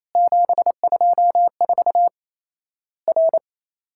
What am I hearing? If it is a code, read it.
Morse code, heard as a keyed tone, about 28 wpm, 700 Hz. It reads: 724 R
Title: Worked All Europe DX Contest - QTC Exchange - 28wpm